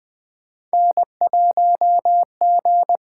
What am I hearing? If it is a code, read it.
Morse code, heard as a keyed tone, about 20 wpm, 700 Hz. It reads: N1G
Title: USA Call Signs - 8x Character Spacing - 20wpm